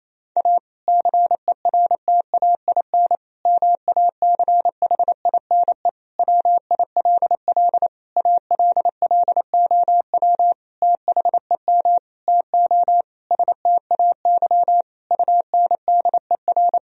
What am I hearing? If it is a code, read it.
Morse code, heard as a keyed tone, about 27 wpm, 700 Hz. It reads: A CERTAIN MACHINE WILL ALLOW THEM TO STAY UNDER
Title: Sentences from Top 1000 Words - Review Part 2 - 28wpm